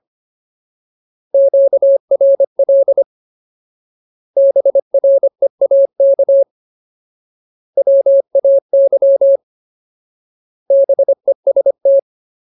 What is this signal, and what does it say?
Morse code, heard as a keyed tone, about 25 wpm, 550 Hz. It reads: QRL BREAK WAY BEST
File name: Ultimate Word List / 25WPM Ultimate